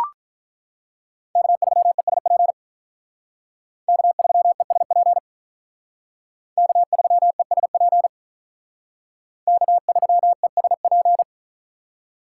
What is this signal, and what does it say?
Morse code, heard as a keyed tone, about 43 wpm, 700 Hz. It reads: K3ESP K3ESP K3ESP K3ESP
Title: Speed Racing - USA 1x3 Call Signs 35wpm